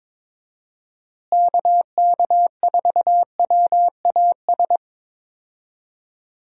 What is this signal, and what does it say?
Morse code, heard as a keyed tone, about 22 wpm, 700 Hz. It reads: KK4WAS
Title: USA Call Signs - 6 Characters - Repeated 3x - 22wpm